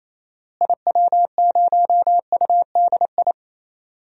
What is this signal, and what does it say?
Morse code, heard as a keyed tone, about 28 wpm, 700 Hz. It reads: IW0UDI
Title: DXCC-100 Call Signs - 6 Characters - Repeated 2x - 28wpm